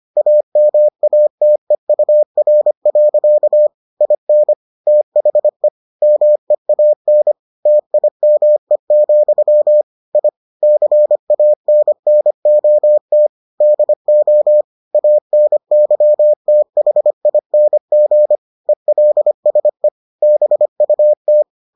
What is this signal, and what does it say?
Morse code, heard as a keyed tone, about 25 wpm, 600 Hz. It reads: AMATEUR. IN THE MEAN TIME, I CANNOT DO ANYTHING ELSE BUT